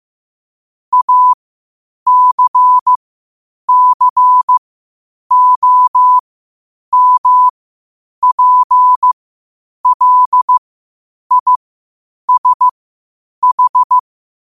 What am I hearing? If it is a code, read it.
Morse code, heard as a keyed tone, about 15 wpm, spaced as for 8 wpm, 1000 Hz. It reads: ACCOMPLISH
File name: qotdfiles / 8wpm